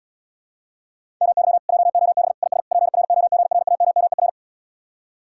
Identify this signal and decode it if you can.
Morse code, heard as a keyed tone, about 80 wpm, 700 Hz. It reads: MY JOB IS PROGRAMMER